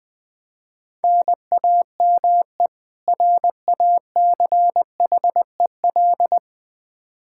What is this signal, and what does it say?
Morse code, heard as a keyed tone, about 21 wpm, 700 Hz. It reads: NAME RACHEL
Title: Practice QSO - Exchange 1 - Beginning - 20wpm